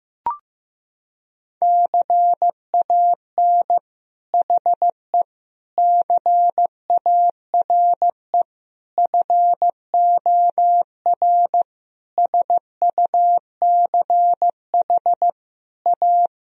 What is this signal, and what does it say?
Morse code, heard as a keyed tone, about 15 wpm, 700 Hz. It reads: CAN HE CARE FOR SUCH A
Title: Sentences from Top 1000 Words - Review Part 1 - 15wpm